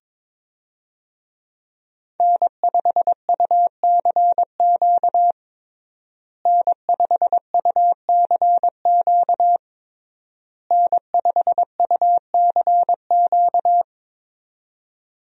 Morse code, heard as a keyed tone, about 21 wpm, 700 Hz. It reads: N5UCQ N5UCQ N5UCQ
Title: USA Call Signs - 5 Characters - Repeated 3x - 22wpm